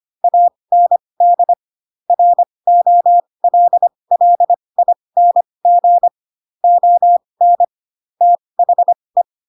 Morse code, heard as a keyed tone, about 25 wpm, spaced as for 20 wpm, 700 Hz. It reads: AND ROLLING ON THE
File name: Pinocchio / Pinocchio ch7 15 25 ik1hge